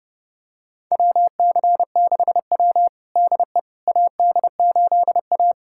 Morse code, heard as a keyed tone, about 30 wpm, 700 Hz. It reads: WC6W DE AD8A
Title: USA Callsign DE Callsign - 4 Characters - 30wpm